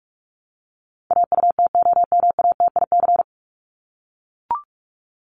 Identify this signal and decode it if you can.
Morse code, heard as a keyed tone, about 44 wpm, 700 Hz. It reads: AUTOMATIC
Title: Top 1751-2000 Words 45wpm